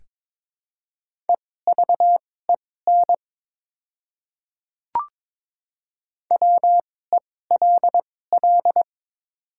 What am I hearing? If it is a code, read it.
Morse code, heard as a keyed tone, about 22 wpm, spaced as for 16 wpm, 700 Hz. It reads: EVEN WELL
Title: 2x Character Spacing - Top 200 Words - 22wpm